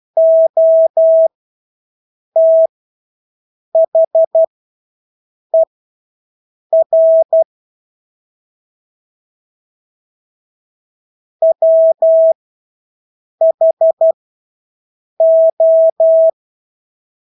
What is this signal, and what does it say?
Morse code, heard as a keyed tone, about 12 wpm, spaced as for 6 wpm, 650 Hz. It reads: OTHER WHO
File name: Engelske ord 12-6 wpm